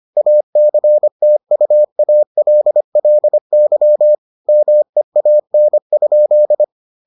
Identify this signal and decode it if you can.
Morse code, heard as a keyed tone, about 25 wpm, 600 Hz. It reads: ACTUALLY MEAN?